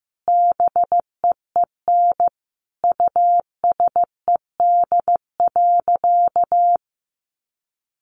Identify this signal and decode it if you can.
Morse code, heard as a keyed tone, about 15 wpm, 700 Hz. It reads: BEEN USED.
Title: Sentences from Top 700 Words 15wpm